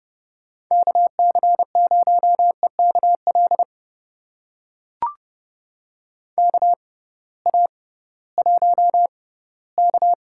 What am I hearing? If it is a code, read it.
Morse code, heard as a keyed tone, about 31 wpm, spaced as for 18 wpm, 700 Hz. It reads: KC0EKL KA1K
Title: USA Call Signs - 6 Characters - 6x Character Spacing - 30wpm